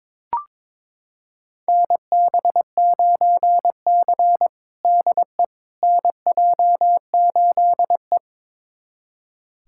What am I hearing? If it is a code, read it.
Morse code, heard as a keyed tone, about 22 wpm, 700 Hz. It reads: NB9C DE NJ8E
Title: USA Callsign DE Callsign - 4 Characters - 22wpm